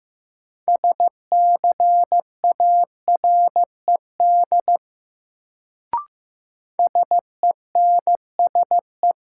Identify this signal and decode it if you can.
Morse code, heard as a keyed tone, about 15 wpm, 700 Hz. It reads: SCARED SENSE
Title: Sentences from Top 700 Words - New Word Drill - 15wpm